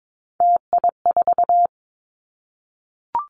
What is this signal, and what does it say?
Morse code, heard as a keyed tone, about 22 wpm, 700 Hz. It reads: TI4
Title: Callsign Prefixes for Top 100 DXCC - 22wpm